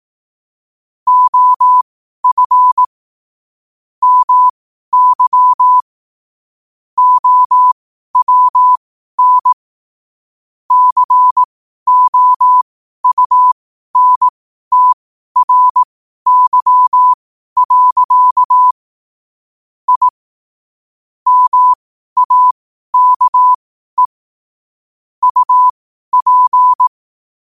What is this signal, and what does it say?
Morse code, heard as a keyed tone, about 18 wpm, spaced as for 12 wpm, 1000 Hz. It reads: OF MY OWN COUNTRY. I MAKE UP
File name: qotdfiles / 12wpm